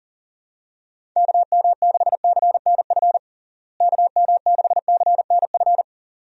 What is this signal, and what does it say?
Morse code, heard as a keyed tone, about 37 wpm, 700 Hz. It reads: KM6CNF KM6CNF
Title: USA Call Signs - 6 Characters - Repeated 2x - 40wpm